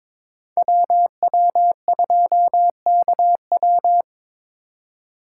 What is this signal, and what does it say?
Morse code, heard as a keyed tone, about 25 wpm, 700 Hz. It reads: WW2KW
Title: Speed Racing - USA 2x2 Call Signs 22wpm